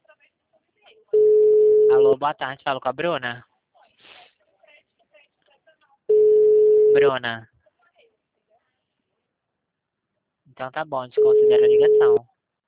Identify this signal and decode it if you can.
Ringback tone (Europe, 425 Hz), with speech.